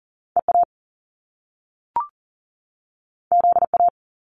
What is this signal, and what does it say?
Morse code, heard as a keyed tone, about 42 wpm, 700 Hz. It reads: EA ZA
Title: Top Two Letter Combinations in Words 40wpm